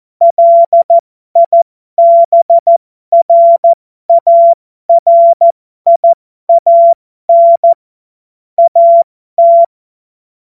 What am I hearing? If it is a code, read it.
Morse code, heard as a keyed tone, about 14 wpm, spaced as for 12 wpm, 680 Hz. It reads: LIBRARIAN AT
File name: Morsekurs / Witze